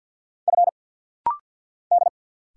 Morse code, heard as a keyed tone, about 48 wpm, 700 Hz. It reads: F D
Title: Single Letter-Number - Rapid-Fire 50wpm